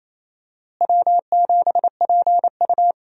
Morse code, heard as a keyed tone, about 28 wpm, 700 Hz. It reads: W7PU
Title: USA Call Signs - 4 Characters - Repeated 2x - 28wpm